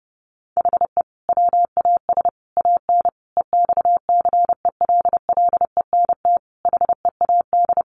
Morse code, heard as a keyed tone, about 30 wpm, 700 Hz. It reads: HE WAS AN EXCELLENT HEAD
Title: Sentences from Top 1200 Words 30wpm